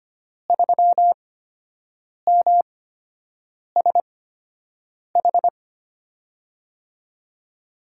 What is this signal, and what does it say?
Morse code, heard as a keyed tone, about 25 wpm, spaced as for 7 wpm, 700 Hz. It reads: 3MSH
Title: USA Call Signs - 5 Characters - 8x Character Spacing - 25wpm